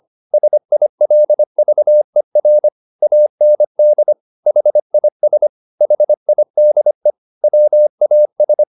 Morse code, heard as a keyed tone, about 25 wpm, 600 Hz. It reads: SILVER AND HIS HIDE WAS